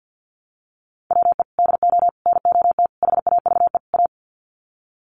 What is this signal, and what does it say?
Morse code, heard as a keyed tone, about 50 wpm, 700 Hz. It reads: WE DO NOT HAVE A
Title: Sets of 5 Words 50wpm